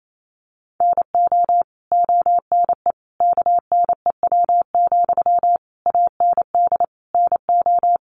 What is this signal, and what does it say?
Morse code, heard as a keyed tone, about 28 wpm, 700 Hz. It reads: NO ONE KNEW, AND NO
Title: Sentences from Top 1400 Words 28wpm